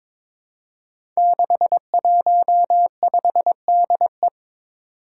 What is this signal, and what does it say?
Morse code, heard as a keyed tone, about 22 wpm, 700 Hz. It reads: 615DE
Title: Five Challenging Letter-Number Combinations 22wpm